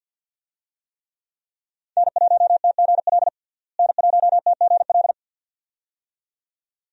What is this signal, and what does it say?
Morse code, heard as a keyed tone, about 51 wpm, 700 Hz. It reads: N1TGL N1TGL
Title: USA Call Signs - 5 Characters - Repeated 2x - 50wpm